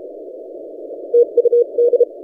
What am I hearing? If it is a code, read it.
Morse code, heard as a keyed tone, about 32 wpm, 480 Hz. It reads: TUD
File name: posts / vk9ec-40cw- -1410z